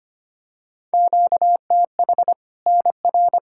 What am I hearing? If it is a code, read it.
Morse code, heard as a keyed tone, about 25 wpm, 700 Hz. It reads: QTH NR
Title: Practice QSO - Exchange 1 - Beginning - 25wpm